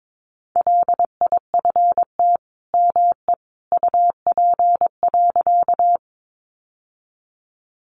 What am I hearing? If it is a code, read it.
Morse code, heard as a keyed tone, about 22 wpm, 700 Hz. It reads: LIFT ME UP.
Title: Sentences from Top 1000 Words 22wpm